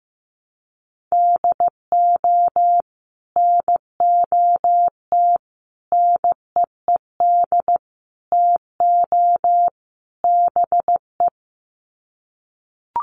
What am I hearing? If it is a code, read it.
Morse code, heard as a keyed tone, about 15 wpm, 700 Hz. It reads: DO NOT NEED TO BE
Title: Sets of 5 Words Encore 15wpm